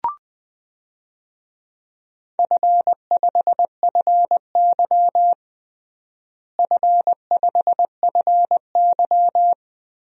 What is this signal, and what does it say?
Morse code, heard as a keyed tone, about 20 wpm, 700 Hz. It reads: F5FY F5FY
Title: DXCC-100 Call Signs - 4 Characters - Repeated 2x - 20wpm